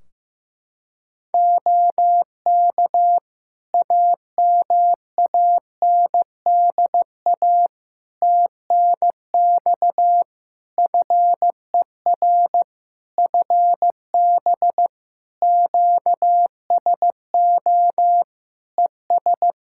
Morse code, heard as a keyed tone, about 15 wpm, 700 Hz. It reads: OK AMANDA TNX FER FB QSO ES
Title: K1USN SST QSO Practice 15wpm